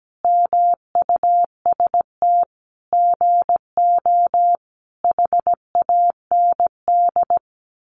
Morse code, heard as a keyed tone, about 17 wpm, 700 Hz. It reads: MUST GO HAND
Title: Sentences from Top 1200 Words 17wpm